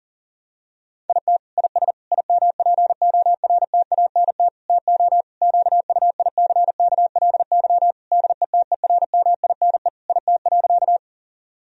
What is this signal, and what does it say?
Morse code, heard as a keyed tone, about 37 wpm, 700 Hz. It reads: IT IS IMPORTANT TO QUICKLY DETERMINE IT.
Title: Sentences from Top 1000 Words - Review Part 4 - 40wpm